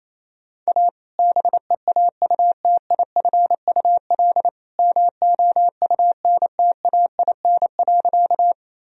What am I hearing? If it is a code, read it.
Morse code, heard as a keyed tone, about 28 wpm, 700 Hz. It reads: A BEAUTIFUL MOUNTAIN.
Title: Sentences from Top 1000 Words - Review Part 1 - 28wpm